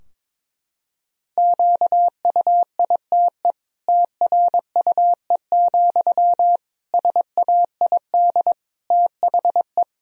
Morse code, heard as a keyed tone, about 21 wpm, 700 Hz. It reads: QUITE TRUE, SAID THE
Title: Sentences from Top 1000 Words - Review Part 3 - 22wpm